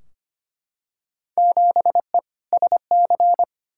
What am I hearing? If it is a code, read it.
Morse code, heard as a keyed tone, about 25 wpm, 700 Hz. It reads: 7E SC
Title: Field Day Exchanges Encore - 25wpm